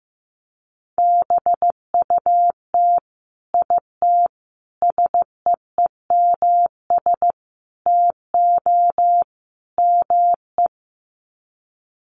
Morse code, heard as a keyed tone, about 15 wpm, 700 Hz. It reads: BUT IT SEEMS TO ME
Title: Sets of 5 Words Encore2 15wpm